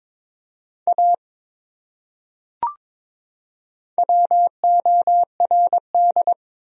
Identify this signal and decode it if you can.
Morse code, heard as a keyed tone, about 22 wpm, 700 Hz. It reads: A WORD
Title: Top 100 Words 22wpm